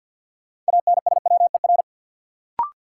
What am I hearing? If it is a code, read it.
Morse code, heard as a keyed tone, about 50 wpm, 700 Hz. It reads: ANSWER
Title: Sentences from Top 300 Words - New Word Drill - 50wpm